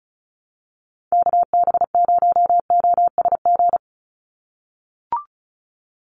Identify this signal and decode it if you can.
Morse code, heard as a keyed tone, about 34 wpm, 700 Hz. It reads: KB0OSG
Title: USA Call Signs 35wpm